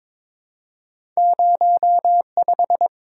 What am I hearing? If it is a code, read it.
Morse code, heard as a keyed tone, about 22 wpm, 700 Hz. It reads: 05
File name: Two Letter-Number Combinations - 22wpm